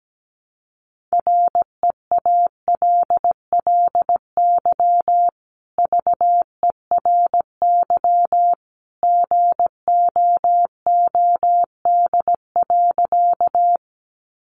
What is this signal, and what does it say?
Morse code, heard as a keyed tone, about 17 wpm, 700 Hz. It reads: REALLY VERY GOOD.
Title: Sentences from Top 600 Words 17wpm